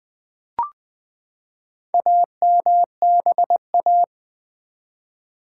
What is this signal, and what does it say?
Morse code, heard as a keyed tone, about 20 wpm, 700 Hz. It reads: AMBA
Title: Build A Word - 4 Letters At A Time - 10-Letter Words - 20wpm